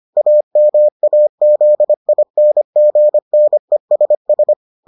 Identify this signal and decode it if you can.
Morse code, heard as a keyed tone, about 25 wpm, 600 Hz. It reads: AMAZINGNESS